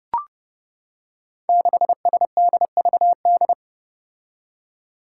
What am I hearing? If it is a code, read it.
Morse code, heard as a keyed tone, about 32 wpm, 700 Hz. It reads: 6SDVD
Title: Five Challenging Letter-Number Combinations 30wpm